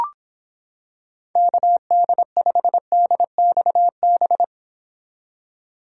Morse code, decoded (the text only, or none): KD5DXB